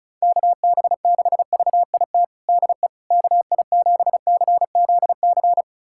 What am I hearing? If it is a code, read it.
Morse code, heard as a keyed tone, about 34 wpm, 700 Hz. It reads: KB6VIT DE KI7CZC
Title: USA Callsign DE Callsign - 6 Characters - 35wpm